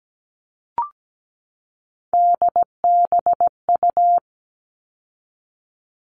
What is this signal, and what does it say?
Morse code, heard as a keyed tone, about 17 wpm, 700 Hz. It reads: DBU
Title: All Cut Numbers - 3-Digit 17wpm